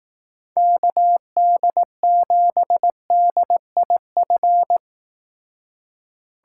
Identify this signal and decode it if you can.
Morse code, heard as a keyed tone, about 18 wpm, 700 Hz. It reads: KD7DIF
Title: Speed Racing - USA 2x3 Call Signs 15wpm